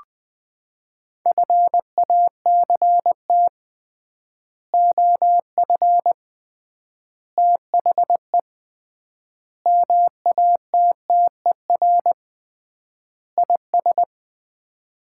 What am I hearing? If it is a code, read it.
Morse code, heard as a keyed tone, about 20 wpm, 700 Hz. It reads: FACT OF THE MATTER IS
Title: 3x Word-Spacing - Sets of 5 Words - 20wpm